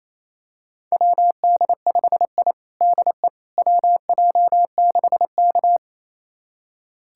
Morse code, heard as a keyed tone, about 28 wpm, 700 Hz. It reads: WD5I DE WJ6K
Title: USA Callsign DE Callsign - 4 Characters - 28wpm